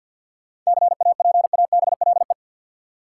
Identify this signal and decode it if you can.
Morse code, heard as a keyed tone, about 50 wpm, 700 Hz. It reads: CAPABLE
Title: Sentences from Top 1200 Words - New Word Drill Alt - 50wpm